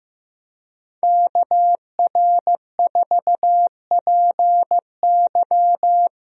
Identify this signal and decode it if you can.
Morse code, heard as a keyed tone, about 15 wpm, 700 Hz. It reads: KR4PY
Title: USA Call Signs - 5 Characters - Repeated 2x - 15wpm